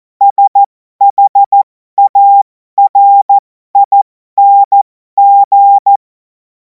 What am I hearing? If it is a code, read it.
Morse code, heard as a keyed tone, about 14 wpm, spaced as for 12 wpm, 800 Hz. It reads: SHARING
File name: Morsekurs / Witze